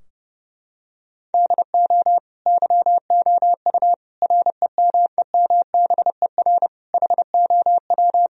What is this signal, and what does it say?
Morse code, heard as a keyed tone, about 30 wpm, 700 Hz. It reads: DO YOU REMEMBER HOW
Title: Sentences from Top 1000 Words - Review Part 3 - 30wpm